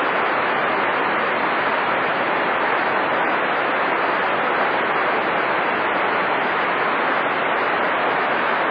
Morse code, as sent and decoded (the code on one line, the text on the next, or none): none